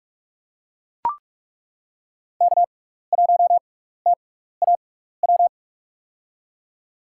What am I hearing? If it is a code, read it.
Morse code, heard as a keyed tone, about 43 wpm, spaced as for 15 wpm, 700 Hz. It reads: K1TAW
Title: USA Call Signs - 5 Characters - 6x Character Spacing - 45wpm